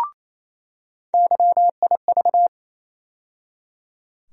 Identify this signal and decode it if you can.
Morse code, heard as a keyed tone, about 27 wpm, 700 Hz. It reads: YIV
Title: USA Call Sign Postfixes - 3 Letters 28wpm